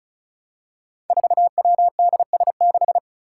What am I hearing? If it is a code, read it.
Morse code, heard as a keyed tone, about 34 wpm, 700 Hz. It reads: 4WDS6
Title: Five Challenging Letter-Number Combinations 35wpm